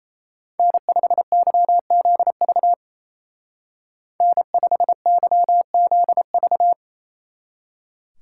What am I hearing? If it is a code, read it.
Morse code, heard as a keyed tone, about 33 wpm, 700 Hz. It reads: N5YZV N5YZV
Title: Speed Racing - USA 1x3 Call Signs 28wpm